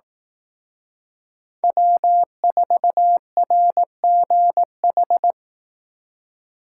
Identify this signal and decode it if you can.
Morse code, heard as a keyed tone, about 18 wpm, 700 Hz. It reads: W4RGH
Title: Speed Racing - USA 1x3 Call Signs 15wpm